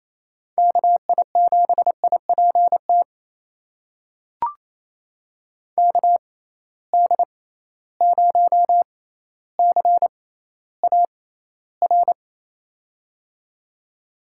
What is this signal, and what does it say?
Morse code, heard as a keyed tone, about 28 wpm, spaced as for 14 wpm, 700 Hz. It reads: KI7IPT KD0CAR
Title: USA Call Signs - 6 Characters - 6x Character Spacing - 28wpm